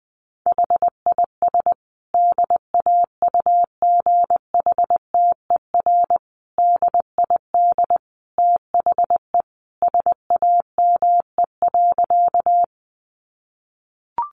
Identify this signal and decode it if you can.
Morse code, heard as a keyed tone, about 20 wpm, 700 Hz. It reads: HIS DAUGHTER DID THE SAME.
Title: Sentences from Top 1500 Words 20wpm